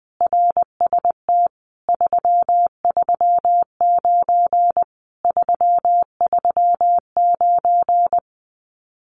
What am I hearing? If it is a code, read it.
Morse code, heard as a keyed tone, about 21 wpm, 700 Hz. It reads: RST 339 339
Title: Common QSO Phrases Encore 20wpm